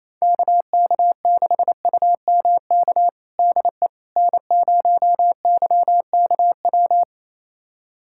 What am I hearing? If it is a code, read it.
Morse code, heard as a keyed tone, about 28 wpm, 700 Hz. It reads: KK6UMK DE N0YKW
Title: USA Callsign DE Callsign 28wpm